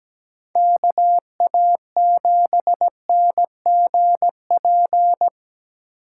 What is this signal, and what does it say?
Morse code, heard as a keyed tone, about 20 wpm, 700 Hz. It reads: KA7NGP
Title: Speed Racing - USA 2x3 Call Signs 17wpm